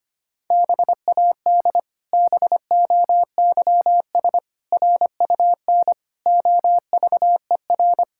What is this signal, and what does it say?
Morse code, heard as a keyed tone, about 25 wpm, 700 Hz. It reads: BAD BOYS RUN OVER
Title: Pangram Challenge v2 - 25wpm